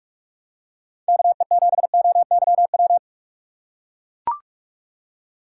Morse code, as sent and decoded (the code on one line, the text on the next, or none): -.- . --... --- -.-- .--
KE7OYW